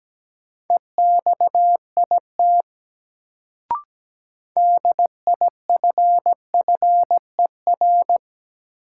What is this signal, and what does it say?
Morse code, heard as a keyed tone, about 17 wpm, 700 Hz. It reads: EXIT DIFFER
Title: Sentences from Top 1100 Words - New Word Drill Alt - 17wpm